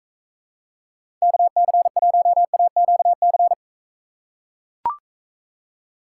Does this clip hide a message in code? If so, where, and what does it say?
Yes: Morse code from 1.2 to 3.5 s, heard as a keyed tone, about 40 wpm, 700 Hz. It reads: KC1AQC